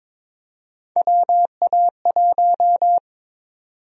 Morse code, heard as a keyed tone, about 22 wpm, 700 Hz. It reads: WA1
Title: USA Call Sign Prefixes - 22wpm